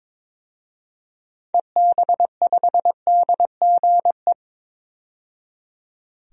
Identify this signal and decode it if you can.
Morse code, heard as a keyed tone, about 22 wpm, 700 Hz. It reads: EB5DGE
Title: DXCC-100 Call Signs - 6 Characters - 22wpm